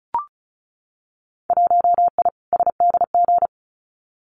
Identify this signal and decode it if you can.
Morse code, heard as a keyed tone, about 35 wpm, 700 Hz. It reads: 1I SDG
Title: Winter Field Day Exchanges 35wpm